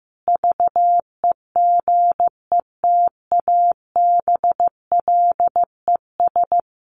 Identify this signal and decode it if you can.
Morse code, heard as a keyed tone, about 15 wpm, 700 Hz. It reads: VEGETABLES
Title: Easy Phrases 15wpm